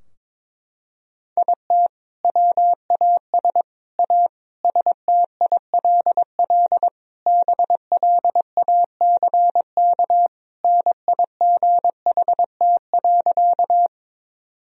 Morse code, heard as a keyed tone, about 22 wpm, 700 Hz. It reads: IT WAS A STILL BLACK NIGHT.
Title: Sentences from Top 1000 Words - Review Part 2 - 22wpm